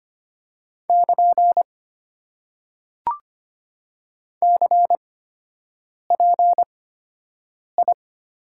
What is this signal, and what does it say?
Morse code, heard as a keyed tone, about 25 wpm, spaced as for 7 wpm, 700 Hz. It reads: <KN> CPI
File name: Top 90 QSO Elements - 8x Character Spacing - 25wpm